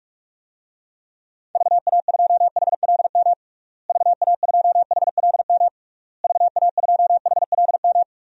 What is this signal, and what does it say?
Morse code, heard as a keyed tone, about 41 wpm, 700 Hz. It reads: VA2HLM VA2HLM VA2HLM
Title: DXCC-100 Call Signs - 6 Characters - Repeated 3x - 45wpm